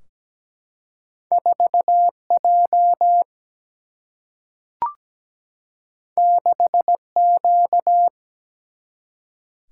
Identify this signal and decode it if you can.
Morse code, heard as a keyed tone, about 17 wpm, 700 Hz. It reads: 4J 6Q
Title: Single-Letter Single-Number 17wpm